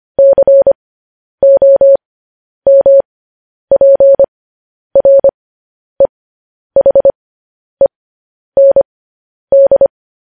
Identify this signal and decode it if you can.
Morse code, heard as a keyed tone, about 25 wpm, spaced as for 10 wpm, 550 Hz. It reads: COMPREHEND